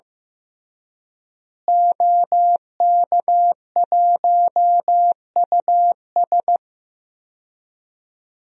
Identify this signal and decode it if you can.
Morse code, heard as a keyed tone, about 15 wpm, 700 Hz. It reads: OK1US
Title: DXCC-100 Call Signs - 5 Characters - Repeated 3x - 15wpm